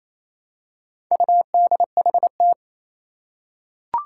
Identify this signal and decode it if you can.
Morse code, heard as a keyed tone, about 27 wpm, 700 Hz. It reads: UDHT
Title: Four Challenging Letter-Number Combinations 28wpm